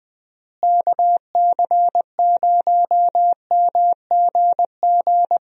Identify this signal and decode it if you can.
Morse code, heard as a keyed tone, about 20 wpm, 700 Hz. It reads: KC0MGG
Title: USA Call Signs - 6 Characters - Repeated 2x - 20wpm